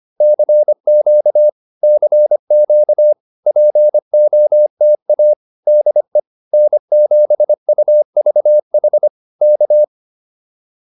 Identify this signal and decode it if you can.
Morse code, heard as a keyed tone, about 25 wpm, 600 Hz. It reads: CQ CQ POTA DE N7UVH K